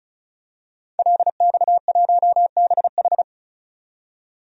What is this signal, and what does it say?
Morse code, heard as a keyed tone, about 34 wpm, 700 Hz. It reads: LX1BH
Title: DXCC-100 Call Signs - 5 Characters - 35wpm